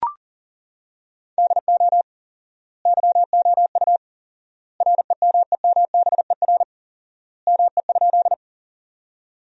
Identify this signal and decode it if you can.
Morse code, heard as a keyed tone, about 40 wpm, 700 Hz. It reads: DO YOU REMEMBER ME?
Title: 4x Word-Spacing - Sentences from Top 400 Words - 40wpm